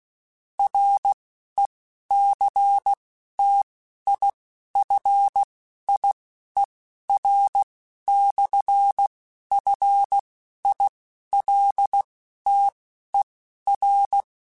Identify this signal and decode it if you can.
Morse code, heard as a keyed tone, about 16 wpm, spaced as for 11 wpm, 800 Hz. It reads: RECTIFIER/FILTER
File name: Technical Subjects / Technical Subjects-mp3s / Sw Pwr Sup @10